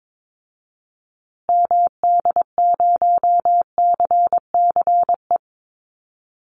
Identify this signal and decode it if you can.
Morse code, heard as a keyed tone, about 22 wpm, 700 Hz. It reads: MD0CCE
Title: Call Signs Repeated 2x - 22wpm